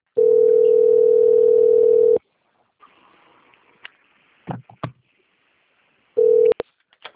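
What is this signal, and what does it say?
Ringback tone (North America), 440 + 480 Hz.
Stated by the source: Outbound telephone call recorded on PBX server Carioca 3.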